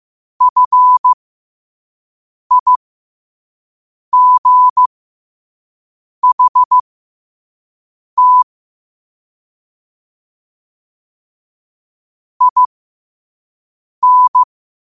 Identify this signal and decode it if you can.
Morse code, heard as a keyed tone, about 15 wpm, spaced as for 5 wpm, 1000 Hz. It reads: FIGHT IN
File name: qotdfiles / 5wpm